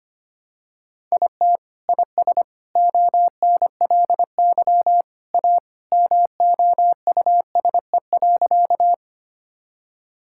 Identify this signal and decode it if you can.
Morse code, heard as a keyed tone, about 25 wpm, 700 Hz. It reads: IT IS ONLY A MOUSE.
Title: Sentences from Top 1000 Words - Review Part 5 - 25wpm